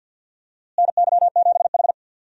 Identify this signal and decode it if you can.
Morse code, heard as a keyed tone, about 53 wpm, 700 Hz. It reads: NY7H
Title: USA Call Signs - 4 Characters - Repeated 2x - 50wpm